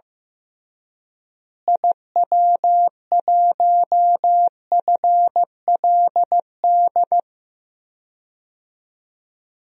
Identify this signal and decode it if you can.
Morse code, heard as a keyed tone, about 15 wpm, 700 Hz. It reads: IW1FLD
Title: DXCC-100 Call Signs - 6 Characters - Repeated 3x - 15wpm